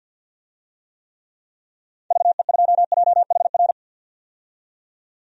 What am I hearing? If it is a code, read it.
Morse code, heard as a keyed tone, about 50 wpm, 700 Hz. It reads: VE2JHR
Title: DXCC-100 Call Signs - 6 Characters - 50wpm